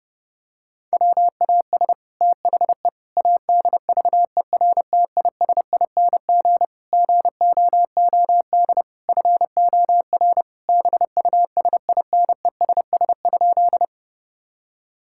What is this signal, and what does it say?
Morse code, heard as a keyed tone, about 30 wpm, 700 Hz. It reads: WAS THE ADVERTISING GOOD FOR BUSINESS?
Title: Sentences from Top 1000 Words - Review Part 6 - 30wpm